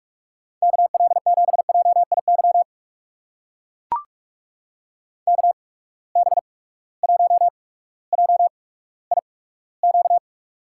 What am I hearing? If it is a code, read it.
Morse code, heard as a keyed tone, about 44 wpm, spaced as for 19 wpm, 700 Hz. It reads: KL7JIY KB1JIQ
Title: USA Call Signs - 6 Characters - 8x Character Spacing - 45wpm